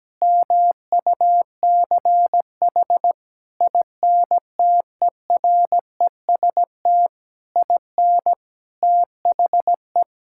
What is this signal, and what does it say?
Morse code, heard as a keyed tone, about 17 wpm, 700 Hz. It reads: MUCH INTEREST IN THE
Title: Sentences from Top 1000 Words - Review Part 5 - 17wpm